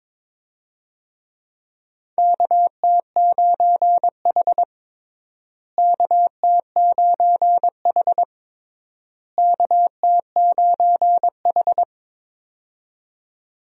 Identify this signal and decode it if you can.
Morse code, heard as a keyed tone, about 22 wpm, 700 Hz. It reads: KT9H KT9H KT9H
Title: USA Call Signs - 4 Characters - Repeated 3x - 22wpm